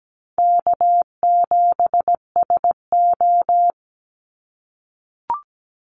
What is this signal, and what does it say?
Morse code, heard as a keyed tone, about 17 wpm, 700 Hz. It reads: K7SO
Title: NA SOTA Activator Callsigns 17wpm